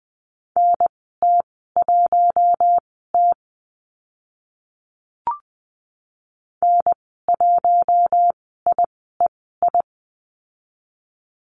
Morse code, heard as a keyed tone, about 20 wpm, spaced as for 14 wpm, 700 Hz. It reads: NT1T N1IEI
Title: 042 Lesson 8g - Number 1 - Call Signs 20wpm